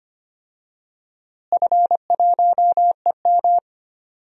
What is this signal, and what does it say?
Morse code, heard as a keyed tone, about 25 wpm, 700 Hz. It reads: F1EM
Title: DXCC-100 Call Signs - 4 Characters - Repeated 3x - 25wpm